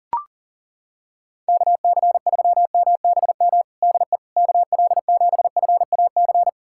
Morse code, heard as a keyed tone, about 37 wpm, 700 Hz. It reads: KC3MBM DE KL7FAC
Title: USA Callsign DE Callsign 40wpm